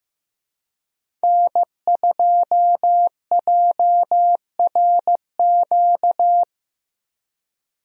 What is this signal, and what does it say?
Morse code, heard as a keyed tone, about 15 wpm, 700 Hz. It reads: N2JRQ
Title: USA Call Signs - 5 Characters - Repeated 3x - 15wpm